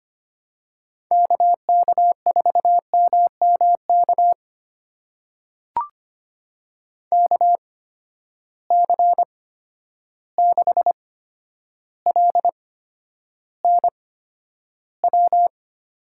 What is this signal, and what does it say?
Morse code, heard as a keyed tone, about 25 wpm, spaced as for 11 wpm, 700 Hz. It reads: KK4MMK KC6LNW
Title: USA Call Signs - 6 Characters - 8x Character Spacing - 25wpm